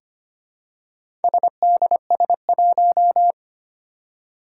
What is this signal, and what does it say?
Morse code, heard as a keyed tone, about 25 wpm, 700 Hz. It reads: SDS1
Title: Four Challenging Letter-Number Combinations 25wpm